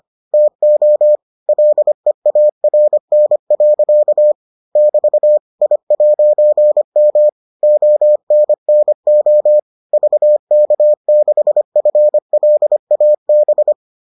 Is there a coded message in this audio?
Yes: Morse code from 0.3 to 13.7 s, heard as a keyed tone, about 25 wpm, 600 Hz. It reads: TO LEARN. <BT> I'M ONNO VK6FLAB